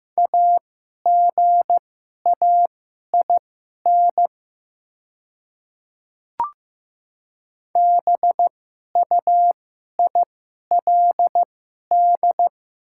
Morse code, heard as a keyed tone, about 15 wpm, spaced as for 11 wpm, 700 Hz. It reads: AGAIN BUILD
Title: 2x Character Spacing - Top 200 Words - 15wpm